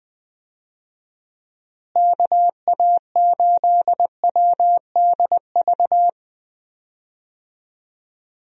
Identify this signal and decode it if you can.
Morse code, heard as a keyed tone, about 20 wpm, 700 Hz. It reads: KA8WDV
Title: USA Call Signs - 6 Characters - Repeated 3x - 20wpm